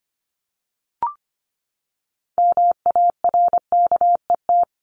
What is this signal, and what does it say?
Morse code, heard as a keyed tone, about 25 wpm, 700 Hz. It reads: MARKET
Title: Top 501-750 Words 25wpm